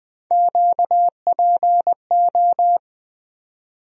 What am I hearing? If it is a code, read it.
Morse code, heard as a keyed tone, about 20 wpm, 700 Hz. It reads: QPO
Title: USA Call Sign Postfixes - 3 Letters 20wpm